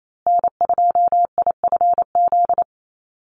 Morse code, heard as a keyed tone, about 28 wpm, 700 Hz. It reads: N2IFZ
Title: USA Call Signs Repeated 3x 28wpm